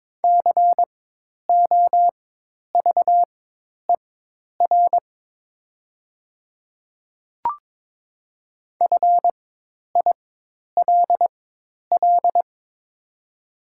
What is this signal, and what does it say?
Morse code, heard as a keyed tone, about 22 wpm, spaced as for 10 wpm, 700 Hz. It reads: COVER FILL
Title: 4x Character Spacing - Top 500 Words - 22wpm